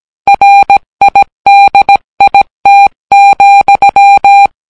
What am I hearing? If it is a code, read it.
Morse code, heard as a keyed tone, about 17 wpm, 800 Hz. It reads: RIDIT,